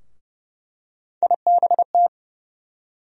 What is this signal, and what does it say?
Morse code, heard as a keyed tone, about 30 wpm, 700 Hz. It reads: IBT
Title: USA Call Sign Postfixes - 3 Letters 30wpm